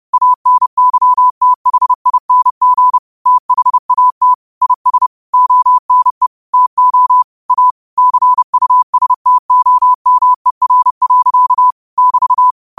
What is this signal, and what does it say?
Morse code, heard as a keyed tone, about 30 wpm, 1000 Hz. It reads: ANYTHING THAT IS ONE TO A CUSTOMER. <BT>